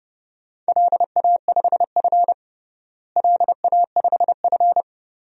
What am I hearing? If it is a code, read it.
Morse code, heard as a keyed tone, about 31 wpm, 700 Hz. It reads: LA5F LA5F
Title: DXCC-100 Call Signs - 4 Characters - Repeated 2x - 30wpm